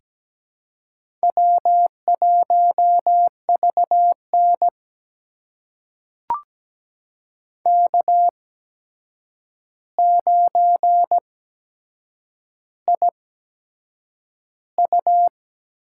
Morse code, heard as a keyed tone, about 17 wpm, spaced as for 7 wpm, 700 Hz. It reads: W1VN K9IU
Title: USA Call Signs - 4 Characters - 8x Character Spacing - 17wpm